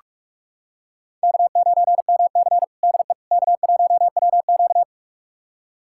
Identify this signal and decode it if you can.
Morse code, heard as a keyed tone, about 46 wpm, 700 Hz. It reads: K9MC DE K1WQ
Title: USA Callsign DE Callsign - 4 Characters - 45wpm